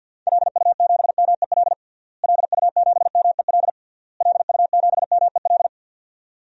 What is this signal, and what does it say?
Morse code, heard as a keyed tone, about 51 wpm, 700 Hz. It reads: LU7MEL LU7MEL LU7MEL
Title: DXCC-100 Call Signs - 6 Characters - Repeated 3x - 50wpm